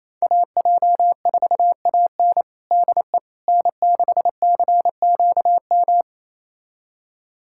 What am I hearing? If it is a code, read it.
Morse code, heard as a keyed tone, about 28 wpm, 700 Hz. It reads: AJ4AN DE N6CQM
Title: USA Callsign DE Callsign - 5 Characters - 28wpm